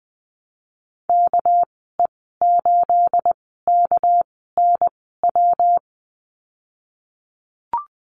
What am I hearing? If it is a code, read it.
Morse code, heard as a keyed tone, about 20 wpm, spaced as for 15 wpm, 700 Hz. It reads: KE8KNW
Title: 213 Lesson 34g - Number 8 - Call Signs 20wpm